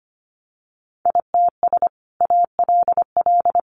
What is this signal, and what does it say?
Morse code, heard as a keyed tone, about 25 wpm, 700 Hz. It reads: ITS ALL
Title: Sentences from Top 600 Words 25wpm